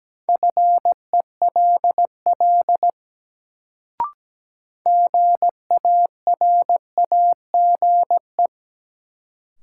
Morse code, heard as a keyed tone, about 17 wpm, 700 Hz. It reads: FELL GARAGE
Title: Sentences from Top 1600 Words - New Word Drill - 17wpm